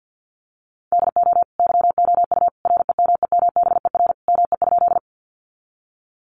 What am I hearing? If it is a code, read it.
Morse code, heard as a keyed tone, about 50 wpm, 700 Hz. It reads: DO YOU REMEMBER ME?
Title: Sentences from Top 400 Words 50wpm